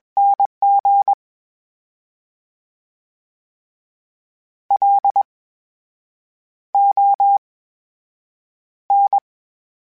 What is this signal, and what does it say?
Morse code, heard as a keyed tone, about 21 wpm, spaced as for 7 wpm, 800 Hz. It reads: NG LON
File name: Special INGs-mp3s / Special INGs-mp3s / VerySlowToFaster Ing @20